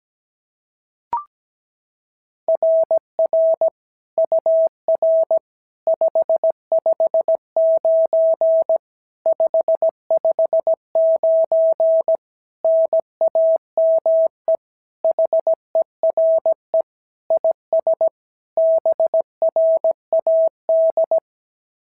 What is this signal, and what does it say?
Morse code, heard as a keyed tone, about 17 wpm, 650 Hz. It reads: RR UR 559 559 NAME HERE IS BRAD
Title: SOTA QSO Practice 17wpm